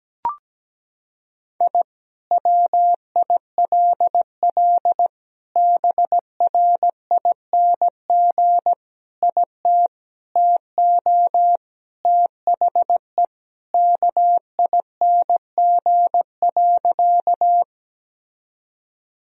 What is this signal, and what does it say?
Morse code, heard as a keyed tone, about 17 wpm, 700 Hz. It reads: I WILL BRING IT TO THE KING.
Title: Sentences from Top 1000 Words - Review Part 3 - 17wpm